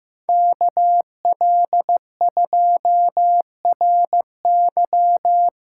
Morse code, heard as a keyed tone, about 15 wpm, 700 Hz. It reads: KL2RY
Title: USA Callsign DE Callsign - 5 Characters - 15wpm